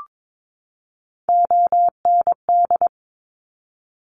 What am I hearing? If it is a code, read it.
Morse code, heard as a keyed tone, about 22 wpm, 700 Hz. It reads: OND
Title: Top Three Letter Combinations in Words 22wpm